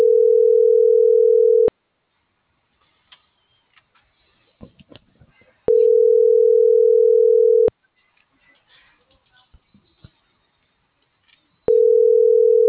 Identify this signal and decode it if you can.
Ringback tone (North America), 440 + 480 Hz.